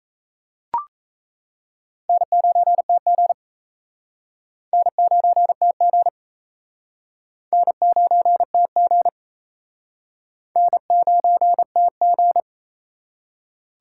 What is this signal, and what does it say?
Morse code, heard as a keyed tone, about 35 wpm, 700 Hz. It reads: N9TG N9TG N9TG N9TG